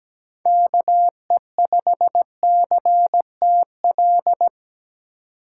Morse code, heard as a keyed tone, about 17 wpm, 700 Hz. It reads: KE5CTL
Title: USA Call Signs - 6 Characters - 3x Character Spacing - 17wpm